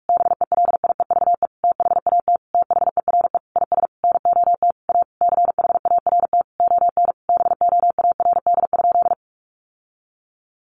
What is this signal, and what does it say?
Morse code, heard as a keyed tone, about 44 wpm, 700 Hz. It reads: BELIEVE THAT THERE IS NOT A CHART ON BOARD?
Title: Sentences from Top 1300 Words 45wpm